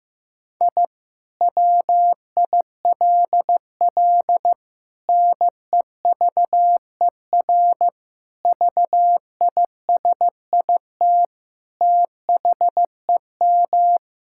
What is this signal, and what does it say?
Morse code, heard as a keyed tone, about 15 wpm, 700 Hz. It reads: I WILL NEVER VISIT THEM
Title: Sentences from Top 1000 Words - Review Part 6 - 15wpm